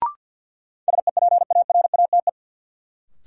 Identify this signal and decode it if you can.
Morse code, heard as a keyed tone, about 52 wpm, 700 Hz. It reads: SEPARATE
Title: Sentences from Top 1000 Words - Review Part 7 - New Word Drill - 50wpm